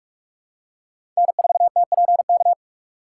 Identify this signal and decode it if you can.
Morse code, heard as a keyed tone, about 46 wpm, 700 Hz. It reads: N4TPK
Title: USA Call Signs - 5 Characters - 2x Character Spacing - 45wpm